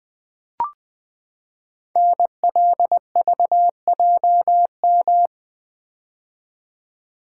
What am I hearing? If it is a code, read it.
Morse code, heard as a keyed tone, about 20 wpm, 700 Hz. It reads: NLVJM
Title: Code Groups - 5 Letters - 20wpm